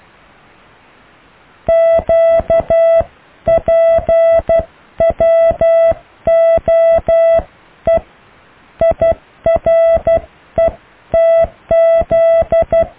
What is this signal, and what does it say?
Morse code, heard as a keyed tone, about 12 wpm, 660 Hz. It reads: QPWOE IRETZ
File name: cw / Alle letters